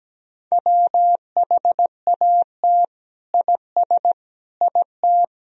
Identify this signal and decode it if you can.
Morse code, heard as a keyed tone, about 17 wpm, 700 Hz. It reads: WHAT IS IT
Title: Sentences from Top 100 Words and Abbreviations 17wpm